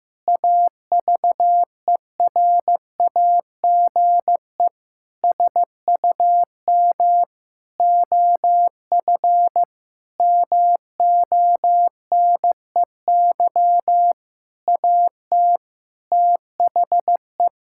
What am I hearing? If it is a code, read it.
Morse code, heard as a keyed tone, about 15 wpm, 700 Hz. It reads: AVERAGE SUM OF MONEY AT THE
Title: Sentences from Top 1000 Words - Review Part 6 - 15wpm